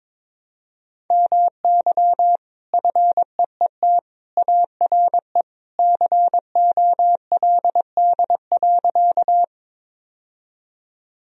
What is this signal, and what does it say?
Morse code, heard as a keyed tone, about 22 wpm, 700 Hz. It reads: MY FEET ARE COLD.
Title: Sentences from Top 1000 Words - Review Part 1 - 22wpm